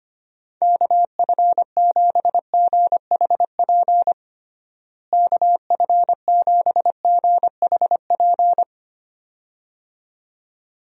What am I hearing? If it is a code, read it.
Morse code, heard as a keyed tone, about 25 wpm, 700 Hz. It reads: KF7GHP KF7GHP
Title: USA Call Signs - 6 Characters - Repeated 2x - 25wpm